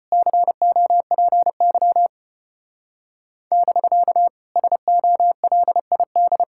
Morse code, heard as a keyed tone, about 30 wpm, 700 Hz. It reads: COPY <BK> SOLID
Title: Speed-Racing Common QSO Phrases - 25wpm